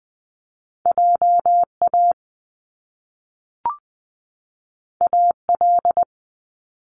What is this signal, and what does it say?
Morse code, heard as a keyed tone, about 20 wpm, 700 Hz. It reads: JA AL
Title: Top Two Letter Combinations in Words 20wpm